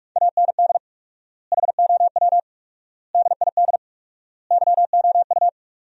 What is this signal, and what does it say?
Morse code, heard as a keyed tone, about 41 wpm, 700 Hz. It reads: AND HOW DID YOU
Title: 4x Word-Spacing - Sentences from Top 200 Words - 45wpm